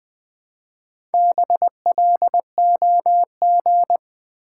Morse code, heard as a keyed tone, about 20 wpm, 700 Hz. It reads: BLOG
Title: Build A Word - 1 Letter At A Time - 3 to 5-Letter Words - 20wpm